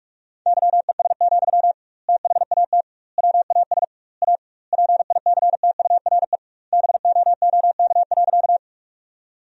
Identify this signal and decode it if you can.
Morse code, heard as a keyed tone, about 46 wpm, 700 Hz. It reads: YES, THAT WAS A PICTURE BOOK.
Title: 2x Word-Spacing - Sentences from Top 500 Words - 45wpm